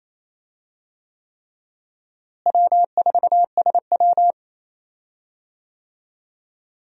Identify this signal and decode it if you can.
Morse code, heard as a keyed tone, about 28 wpm, 700 Hz. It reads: W4SW
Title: USA Call Signs - 4 Characters - Repeated 3x - 28wpm